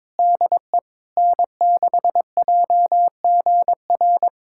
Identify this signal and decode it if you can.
Morse code, heard as a keyed tone, about 22 wpm, 700 Hz. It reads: DE N6JGR
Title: USA Callsign DE Callsign - 5 Characters - 22wpm